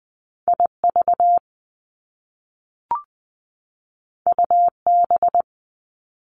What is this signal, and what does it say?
Morse code, heard as a keyed tone, about 20 wpm, 700 Hz. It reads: IV UB
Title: Top Two Letter Combinations in Words 20wpm